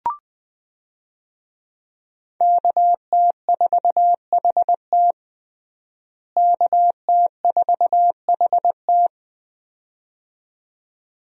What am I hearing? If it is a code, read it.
Morse code, heard as a keyed tone, about 20 wpm, 700 Hz. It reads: KT4HT KT4HT
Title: USA Call Signs - 5 Characters - Repeated 2x - 20wpm